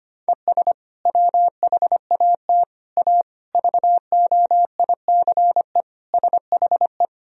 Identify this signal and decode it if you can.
Morse code, heard as a keyed tone, about 25 wpm, 700 Hz. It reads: ES WHAT A VOICE SHE
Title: Sentences from Top 500 Words and Abbreviations 25wpm